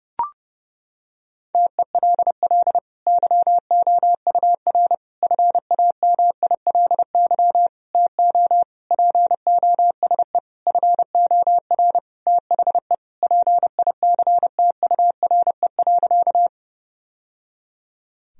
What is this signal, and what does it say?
Morse code, heard as a keyed tone, about 30 wpm, 700 Hz. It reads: TELL YOUR FAMILY TO POSE FOR THE PICTURE.
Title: Sentences from Top 1000 Words - Review Part 2 - 30wpm